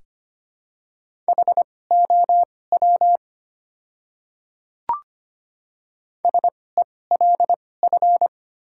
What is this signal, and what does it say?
Morse code, heard as a keyed tone, about 25 wpm, spaced as for 18 wpm, 700 Hz. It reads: HOW SELF
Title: 2x Character Spacing - Top 200 Words - 25wpm